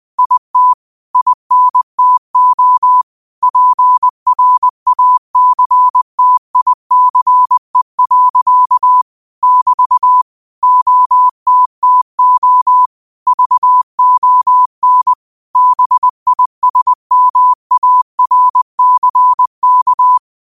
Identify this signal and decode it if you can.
Morse code, heard as a keyed tone, about 20 wpm, 1000 Hz. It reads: IT INTO PRACTICE. <BT> OTTO VON BISMARCK